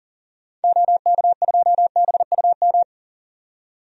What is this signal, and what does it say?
Morse code, heard as a keyed tone, about 40 wpm, 700 Hz. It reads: OK2BUM
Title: DXCC-100 Call Signs - 6 Characters - 40wpm